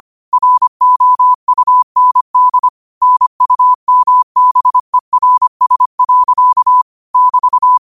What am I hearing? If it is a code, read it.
Morse code, heard as a keyed tone, about 25 wpm, 1000 Hz. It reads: ROUND NUMBERS. <BT>
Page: New quotes every day in morse code at 25 Words per minute.